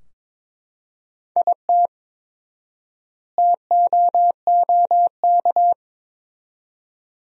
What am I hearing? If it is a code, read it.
Morse code, heard as a keyed tone, about 22 wpm, 700 Hz. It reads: IT TOOK
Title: 4x Word-Spacing - Sentences from Top 300 Words - 22wpm